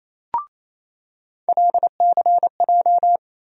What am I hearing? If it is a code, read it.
Morse code, heard as a keyed tone, about 28 wpm, 700 Hz. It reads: LCJ
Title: USA Call Sign Postfixes - 3 Letters 28wpm